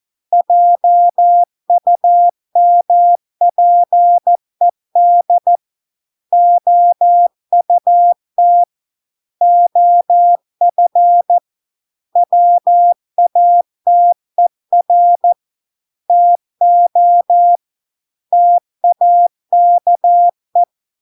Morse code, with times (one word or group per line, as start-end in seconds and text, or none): 0.3-5.6 JUMPED
6.3-8.6 OUT
9.4-11.4 OF
12.2-15.3 WATER
16.1-17.6 TO
18.3-20.6 TAKE